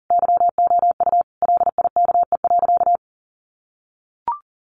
Morse code, heard as a keyed tone, about 40 wpm, 700 Hz. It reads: YOU LIKE.
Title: Sentences from Top 500 Words 40wpm